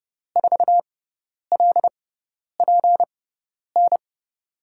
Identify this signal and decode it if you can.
Morse code, heard as a keyed tone, about 30 wpm, spaced as for 10 wpm, 700 Hz. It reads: 4LPN
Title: USA Call Signs - 5 Characters - 6x Character Spacing - 30wpm